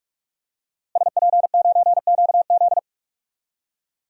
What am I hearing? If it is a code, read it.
Morse code, heard as a keyed tone, about 42 wpm, 700 Hz. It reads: SP9QZ
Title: DXCC-100 Call Signs - 5 Characters - Repeated 3x - 45wpm